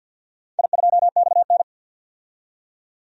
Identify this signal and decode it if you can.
Morse code, heard as a keyed tone, about 51 wpm, 700 Hz. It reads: I2XN
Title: DXCC-100 Call Signs - 4 Characters - 50wpm